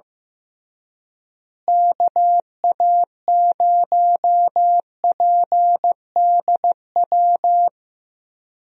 Morse code, heard as a keyed tone, about 15 wpm, 700 Hz. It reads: KA0PDW
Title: USA Call Signs - 6 Characters - Repeated 3x - 15wpm